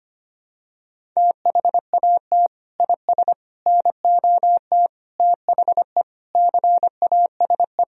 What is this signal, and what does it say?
Morse code, heard as a keyed tone, about 25 wpm, 700 Hz. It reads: THAT IS NOT THE CASE
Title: Sentences from Top 1000 Words - Review Part 2 - 25wpm